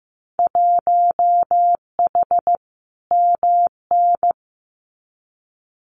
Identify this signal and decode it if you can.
Morse code, heard as a keyed tone, about 15 wpm, 700 Hz. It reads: 1H MN
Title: Winter Field Day Exchanges 15wpm